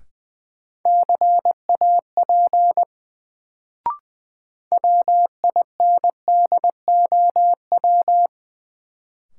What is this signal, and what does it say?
Morse code, heard as a keyed tone, about 20 wpm, 700 Hz. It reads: CAP WINDOW
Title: Sentences from Top 1600 Words - New Word Drill - 20wpm